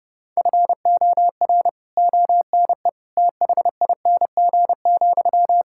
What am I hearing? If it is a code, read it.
Morse code, heard as a keyed tone, about 30 wpm, 700 Hz. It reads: FOR ONE THING,
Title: Sentences from Top 1000 Words - Review Part 6 - 30wpm